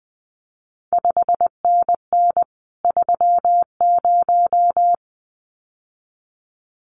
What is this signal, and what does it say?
Morse code, heard as a keyed tone, about 20 wpm, 700 Hz. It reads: 5NN 30
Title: CQ WW DX Contest Exchange Encore - 20wpm